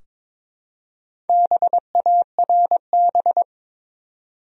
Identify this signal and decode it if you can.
Morse code, heard as a keyed tone, about 22 wpm, 700 Hz. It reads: BARB
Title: USA Common Names - Max 5 Letters - 22wpm